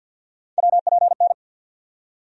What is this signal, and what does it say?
Morse code, heard as a keyed tone, about 49 wpm, 700 Hz. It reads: WPN
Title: USA Call Sign Postfixes - 3 Letters 50wpm